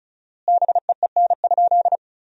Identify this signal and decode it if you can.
Morse code, heard as a keyed tone, about 34 wpm, 700 Hz. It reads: BEEN?
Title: 4x Word-Spacing - Sentences from Top 200 Words - 35wpm